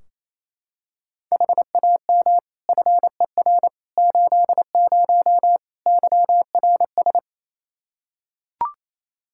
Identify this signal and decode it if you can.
Morse code, heard as a keyed tone, about 28 wpm, 700 Hz. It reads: HAM FER 80 YRS
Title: Practice QSO - Exchange 3 - More Info - 28wpm